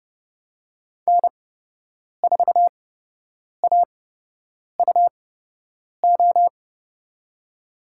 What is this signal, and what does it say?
Morse code, heard as a keyed tone, about 29 wpm, spaced as for 8 wpm, 700 Hz. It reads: N4AUO
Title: USA Call Signs - 5 Characters - 8x Character Spacing - 30wpm